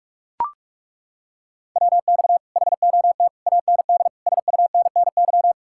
Morse code, heard as a keyed tone, about 49 wpm, 700 Hz. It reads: WX HOT AND SUNNY